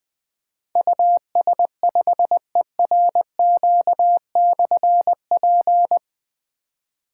Morse code, heard as a keyed tone, about 20 wpm, 700 Hz. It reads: US5ERQ/P
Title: Difficult Call Signs 20wpm